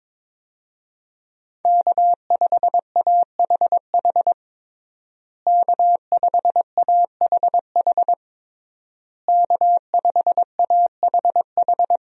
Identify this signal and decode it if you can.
Morse code, heard as a keyed tone, about 22 wpm, 700 Hz. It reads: K5AHH K5AHH K5AHH
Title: USA Call Signs - 5 Characters - Repeated 3x - 22wpm